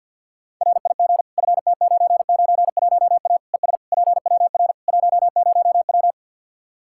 Morse code, heard as a keyed tone, about 46 wpm, 700 Hz. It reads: RIG FT991A ES PWR 10W
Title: K1USN SST QSO Practice 50wpm